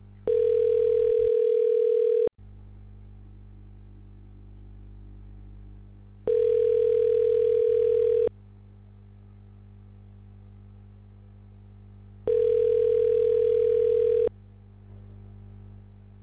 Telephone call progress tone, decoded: ringback tone

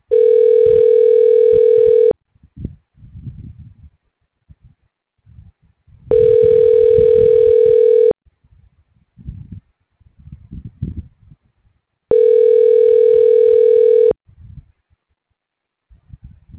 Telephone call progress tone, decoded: ringback tone

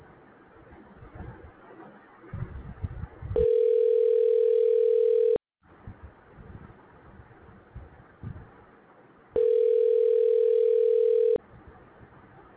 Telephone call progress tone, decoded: ringback tone

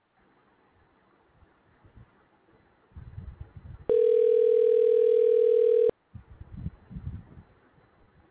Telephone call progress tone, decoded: ringback tone

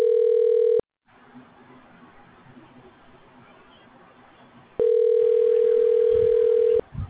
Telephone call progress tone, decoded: ringback tone

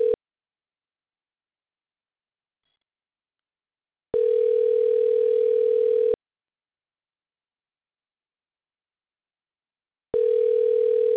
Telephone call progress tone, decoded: ringback tone

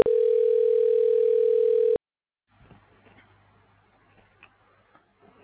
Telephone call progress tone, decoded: ringback tone

